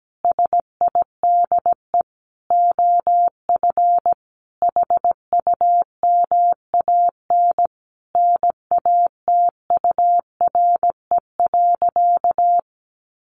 Morse code, heard as a keyed tone, about 17 wpm, 700 Hz. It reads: SIDE OF HUMAN NATURE.
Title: Sentences from Top 700 Words 17wpm